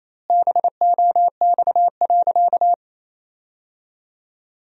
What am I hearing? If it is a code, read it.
Morse code, heard as a keyed tone, about 28 wpm, 700 Hz. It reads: BOX.
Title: 2x Word-Spacing - Sentences from Top 500 Words - 28wpm